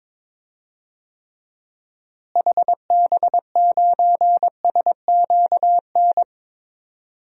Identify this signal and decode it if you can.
Morse code, heard as a keyed tone, about 21 wpm, 700 Hz. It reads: HB9SQN
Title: DXCC-100 Call Signs - 6 Characters - Repeated 2x - 22wpm